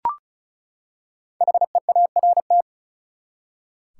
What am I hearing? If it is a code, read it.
Morse code, heard as a keyed tone, about 36 wpm, 700 Hz. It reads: HEART
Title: Five Letter Words - 35wpm